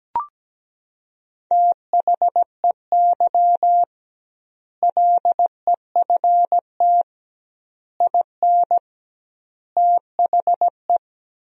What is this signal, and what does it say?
Morse code, heard as a keyed tone, about 17 wpm, 700 Hz. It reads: THEY LEFT IN THE
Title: 2x Word-Spacing - Sentences from Top 400 Words - 17wpm